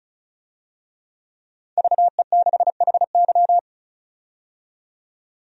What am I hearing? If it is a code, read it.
Morse code, heard as a keyed tone, about 34 wpm, 700 Hz. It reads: VE6HY
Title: DXCC-100 Call Signs - 5 Characters - 35wpm